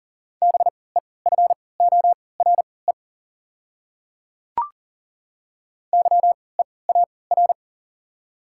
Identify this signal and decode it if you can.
Morse code, heard as a keyed tone, about 40 wpm, spaced as for 23 wpm, 700 Hz. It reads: BEFORE YEAR
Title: 3x Character Spacing - Top 200 Words - 40wpm